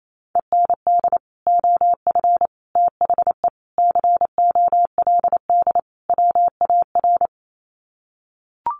Morse code, heard as a keyed tone, about 28 wpm, 700 Hz. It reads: END OF THE COLD WAR
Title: Sets of 5 Words Encore 28wpm